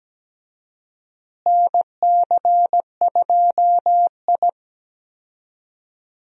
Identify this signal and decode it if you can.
Morse code, heard as a keyed tone, about 17 wpm, 700 Hz. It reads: NC2I
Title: USA Call Signs - 4 Characters - Repeated 2x - 17wpm